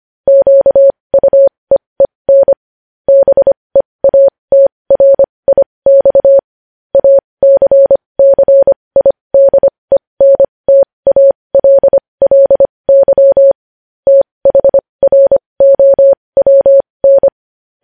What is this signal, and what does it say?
Morse code, heard as a keyed tone, about 25 wpm, spaced as for 20 wpm, 550 Hz. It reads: QUEEN BEATRIX ACCIDENTALLY THROWN
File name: podcast / News headlines-1x-25x20